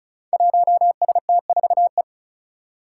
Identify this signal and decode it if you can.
Morse code, heard as a keyed tone, about 36 wpm, 700 Hz. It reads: 1ST4E
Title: Five Challenging Letter-Number Combinations 35wpm